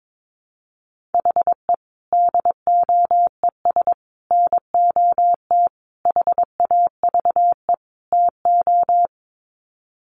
Morse code, heard as a keyed tone, about 22 wpm, 700 Hz. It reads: HE DOES NOT HAVE TO
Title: Sets of 5 Words Encore 22wpm